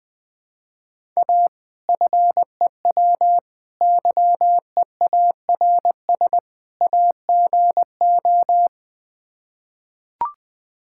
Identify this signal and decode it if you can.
Morse code, heard as a keyed tone, about 20 wpm, 700 Hz. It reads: A FEW YEARS AGO
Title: Build A Phrase - 4-Word Phrases - 20wpm